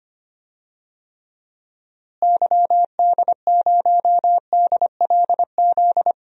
Morse code, heard as a keyed tone, about 25 wpm, 700 Hz. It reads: YD0DLZ
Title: DXCC-100 Call Signs - 6 Characters - Repeated 3x - 25wpm